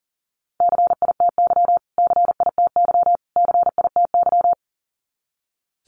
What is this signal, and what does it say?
Morse code, heard as a keyed tone, about 40 wpm, 700 Hz. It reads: CITY CITY CITY
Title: Top 500 Words Repeated 40wpm